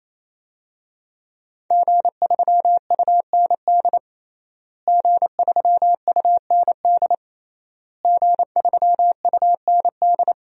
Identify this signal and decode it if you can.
Morse code, heard as a keyed tone, about 28 wpm, 700 Hz. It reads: G3UND G3UND G3UND
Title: DXCC-100 Call Signs - 5 Characters - Repeated 3x - 28wpm